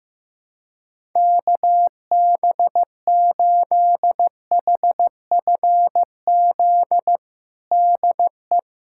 Morse code, heard as a keyed tone, about 15 wpm, 700 Hz. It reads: KB8HFZ DE
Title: USA Callsign DE Callsign - 6 Characters - 15wpm